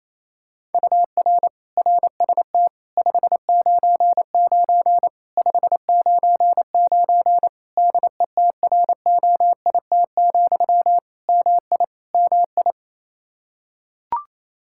Morse code, heard as a keyed tone, about 28 wpm, 700 Hz. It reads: UR RST 599 599 DETROIT, MI MI
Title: Practice QSO - Exchange 1 - Beginning - 28wpm